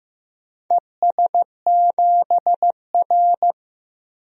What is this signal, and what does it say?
Morse code, heard as a keyed tone, about 15 wpm, 700 Hz. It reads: ES7R
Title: DXCC-100 Call Signs - 4 Characters - Repeated 2x - 15wpm